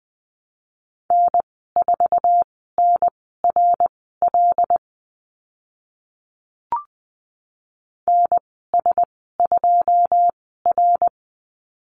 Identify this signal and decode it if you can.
Morse code, heard as a keyed tone, about 20 wpm, spaced as for 14 wpm, 700 Hz. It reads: N4NRL NS2R
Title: 082 Lesson 14g - Number 2 - Call Signs 20wpm